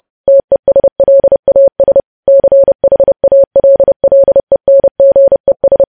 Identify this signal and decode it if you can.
Morse code, heard as a keyed tone, about 30 wpm, 550 Hz. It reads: TESLAS CHALLENGES
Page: news headlines in morse code at 30 wpm